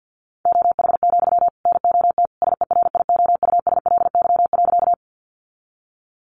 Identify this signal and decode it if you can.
Morse code, heard as a keyed tone, about 51 wpm, 700 Hz. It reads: OH, NOT SERIOUSLY.
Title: Sentences from Top 1200 Words 50wpm